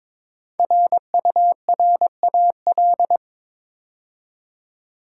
Morse code, heard as a keyed tone, about 22 wpm, 690 Hz. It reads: RURAL
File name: 5 Letter Words - First 3 Letters Given - 22wpm